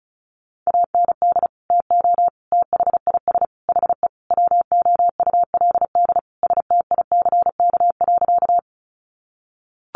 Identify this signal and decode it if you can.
Morse code, heard as a keyed tone, about 34 wpm, 700 Hz. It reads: AND TO THIS HE WOULD STICK.
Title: Sentences from Top 1200 Words 35wpm